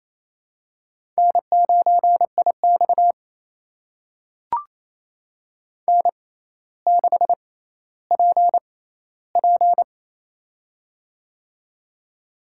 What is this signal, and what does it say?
Morse code, heard as a keyed tone, about 28 wpm, spaced as for 14 wpm, 700 Hz. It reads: N9IX N6PP
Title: USA Call Signs - 4 Characters - 6x Character Spacing - 28wpm